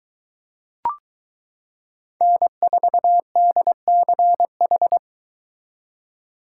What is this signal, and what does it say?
Morse code, heard as a keyed tone, about 22 wpm, 700 Hz. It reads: N4DCH